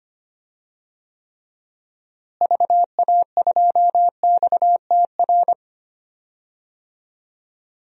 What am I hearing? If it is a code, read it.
Morse code, heard as a keyed tone, about 25 wpm, 700 Hz. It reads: VA2XTR
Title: DXCC-100 Call Signs - 6 Characters - Repeated 3x - 25wpm